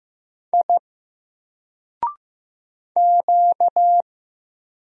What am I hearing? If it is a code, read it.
Morse code, heard as a keyed tone, about 15 wpm, 700 Hz. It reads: I Q
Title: Single Character (Letter:Number:Punctuation:Prosign) 15wpm